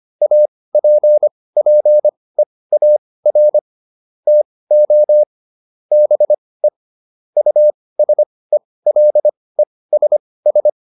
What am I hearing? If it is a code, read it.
Morse code, heard as a keyed tone, about 25 wpm, spaced as for 18 wpm, 600 Hz. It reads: APPEAR TO BE USELESS